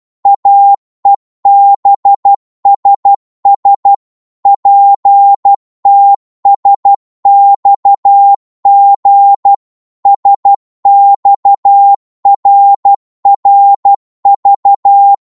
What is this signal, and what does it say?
Morse code, heard as a keyed tone, about 12 wpm, 800 Hz. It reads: AEBSS PTSXG SXRRV
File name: Morse / ltr-12wpm